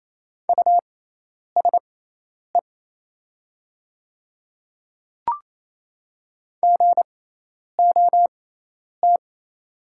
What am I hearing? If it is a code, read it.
Morse code, heard as a keyed tone, about 27 wpm, spaced as for 10 wpm, 700 Hz. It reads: USE GOT
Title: 6x Character Spacing - Top 500 Words - 28wpm